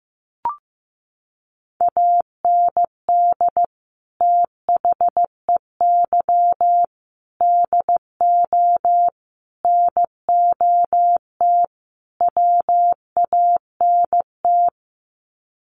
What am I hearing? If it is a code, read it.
Morse code, heard as a keyed tone, about 15 wpm, 700 Hz. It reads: AND THEY DO NOT WANT
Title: Sets of 5 Words Encore2 15wpm